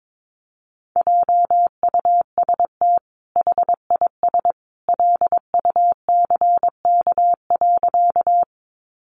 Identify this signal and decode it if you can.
Morse code, heard as a keyed tone, about 22 wpm, 700 Hz. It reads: JUST HIS LUCK.
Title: Sentences from Top 1500 Words 22wpm